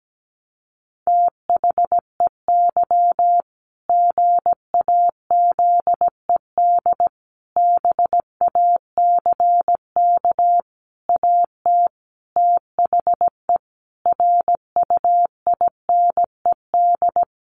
Morse code, heard as a keyed tone, about 17 wpm, 700 Hz. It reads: THEY GAZED BACK AT THE RUINED
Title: A Sound Of Thunder 17wpm